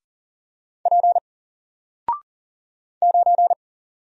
Morse code, heard as a keyed tone, about 40 wpm, 700 Hz. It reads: P 9
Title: Single Character (Letter:Number:Punctuation:Prosign) Alt 40wpm